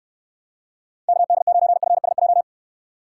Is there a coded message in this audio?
Yes: Morse code from 1.1 to 2.4 s, heard as a keyed tone, about 72 wpm, 700 Hz. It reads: KD9FSJ